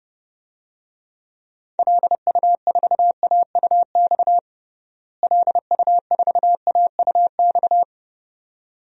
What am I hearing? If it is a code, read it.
Morse code, heard as a keyed tone, about 32 wpm, 700 Hz. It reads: LU4AUX LU4AUX
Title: DXCC-100 Call Signs - 6 Characters - Repeated 2x - 30wpm